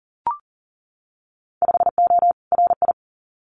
Morse code, heard as a keyed tone, about 40 wpm, 700 Hz. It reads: 5O RI
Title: Winter Field Day Exchanges Encore 40wpm